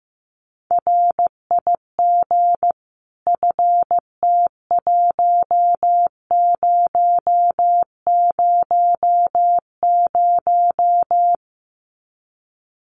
Morse code, heard as a keyed tone, about 15 wpm, 700 Hz. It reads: RIG FT1000
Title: Common QSO Phrases 15wpm